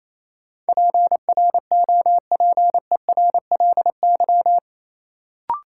Morse code, heard as a keyed tone, about 28 wpm, 700 Hz. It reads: PROPERLY
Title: Sentences from Top 1100 Words - New Word Drill Alt - 28wpm